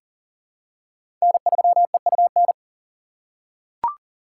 Morse code, heard as a keyed tone, about 38 wpm, 700 Hz. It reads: N3EUN
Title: USA Call Signs - 5 Characters - 3x Character Spacing - 40wpm